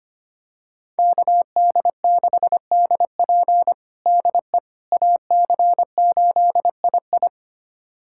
Morse code, heard as a keyed tone, about 25 wpm, 700 Hz. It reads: KD6DP DE AC8II
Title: USA Callsign DE Callsign - 5 Characters - 25wpm